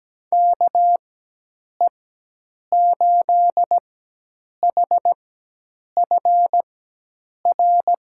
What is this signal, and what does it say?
Morse code, heard as a keyed tone, about 17 wpm, spaced as for 8 wpm, 700 Hz. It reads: KE8HFR
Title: USA Call Signs - 4x Character Spacing - 17wpm